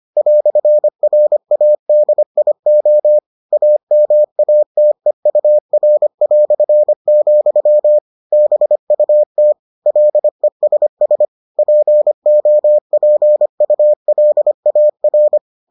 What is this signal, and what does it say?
Morse code, heard as a keyed tone, about 25 wpm, 600 Hz. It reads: "RADIO AMATEUR", BUT LESS POPULAR